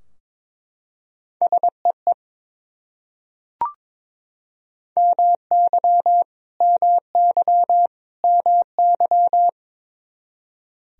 Morse code, heard as a keyed tone, about 22 wpm, 700 Hz. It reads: SEE MY MY MY
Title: Top 100 Words Repeated 22wpm